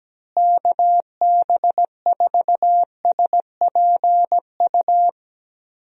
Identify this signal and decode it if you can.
Morse code, heard as a keyed tone, about 17 wpm, 700 Hz. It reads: KB4SPU
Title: USA Call Signs - 6 Characters - Repeated 3x - 17wpm